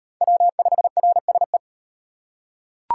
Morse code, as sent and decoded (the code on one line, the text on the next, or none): .-- ..... .-. ... .
W5RSE